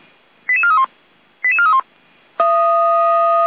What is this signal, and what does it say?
Five-tone selective call (ZVEI), heard as quick runs of five tones, tones of about 70 ms. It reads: 88431 88431
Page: SDR/ZVEI